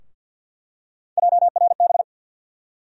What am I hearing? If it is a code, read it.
Morse code, heard as a keyed tone, about 50 wpm, 700 Hz. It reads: JRB
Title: USA Call Sign Postfixes - 3 Letters 50wpm